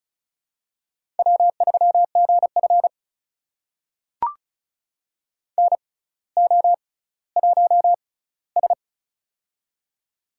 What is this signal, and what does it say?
Morse code, heard as a keyed tone, about 36 wpm, spaced as for 18 wpm, 700 Hz. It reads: W3GF NO1S
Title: USA Call Signs - 6x Character Spacing - 35wpm